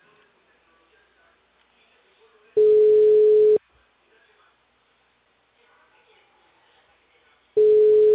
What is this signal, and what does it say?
Ringback tone (Europe, 425 Hz), with speech.